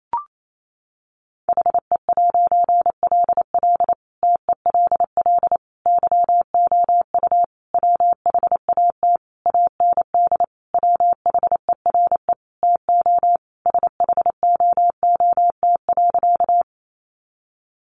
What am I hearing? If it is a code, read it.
Morse code, heard as a keyed tone, about 28 wpm, 700 Hz. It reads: HE'LL TELL YOU WHAT AND WHERE TO SHOOT.
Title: A Sound Of Thunder 28wpm